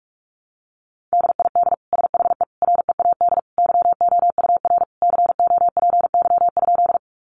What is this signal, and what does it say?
Morse code, heard as a keyed tone, about 44 wpm, 700 Hz. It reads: DID SHE READ YOUR COPY?
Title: Sentences from Top 1400 Words 45wpm